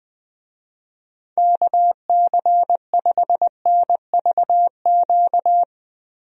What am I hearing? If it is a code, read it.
Morse code, heard as a keyed tone, about 20 wpm, 700 Hz. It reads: KC5NVQ
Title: USA Call Signs - 6 Characters - Repeated 2x - 20wpm